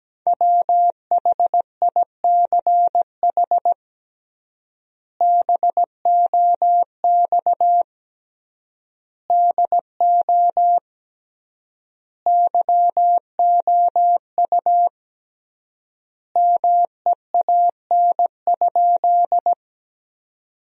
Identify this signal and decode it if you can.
Morse code, heard as a keyed tone, about 17 wpm, 700 Hz. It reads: WHICH BOX DO YOU MEAN?
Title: 3x Word-Spacing - Sentences from Top 500 Words - 17wpm